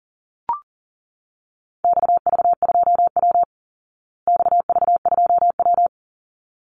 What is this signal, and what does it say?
Morse code, heard as a keyed tone, about 40 wpm, 700 Hz. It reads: XV2W XV2W
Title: Call Signs Repeated 3x - 40wpm